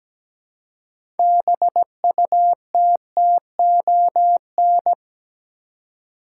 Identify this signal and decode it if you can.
Morse code, heard as a keyed tone, about 17 wpm, 700 Hz. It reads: BUTTON
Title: Six Letter Words - 17wpm